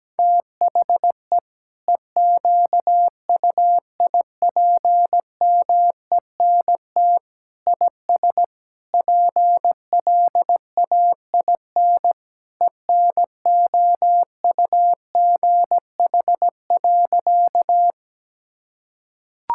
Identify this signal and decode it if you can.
Morse code, heard as a keyed tone, about 17 wpm, 700 Hz. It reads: THE EQUIPMENT IS PLAIN ENOUGH.
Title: Sentences from Top 1000 Words - Review Part 5 - 17wpm